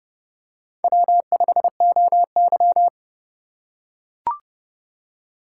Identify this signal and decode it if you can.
Morse code, heard as a keyed tone, about 30 wpm, 700 Hz. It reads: W5OY